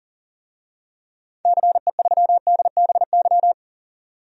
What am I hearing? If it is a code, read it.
Morse code, heard as a keyed tone, about 40 wpm, 700 Hz. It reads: CE3DBY
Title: DXCC-100 Call Signs - 6 Characters - 40wpm